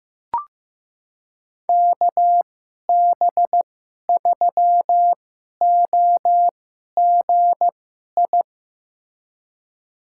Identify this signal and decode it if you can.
Morse code, heard as a keyed tone, about 15 wpm, spaced as for 11 wpm, 700 Hz. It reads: KB3OGI
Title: USA Call Signs - 6 Characters - 2x Character Spacing - 15wpm